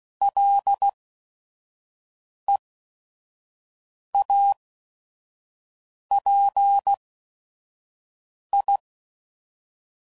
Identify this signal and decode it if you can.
Morse code, heard as a keyed tone, about 16 wpm, spaced as for 5 wpm, 800 Hz. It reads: LEAPI
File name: Suffixes / Special ING's / VerySlowToFaster ING's